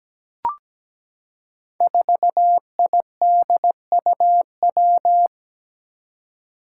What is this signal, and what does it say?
Morse code, heard as a keyed tone, about 17 wpm, 700 Hz. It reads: 4IDUW
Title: Five Challenging Letter-Number Combinations 17wpm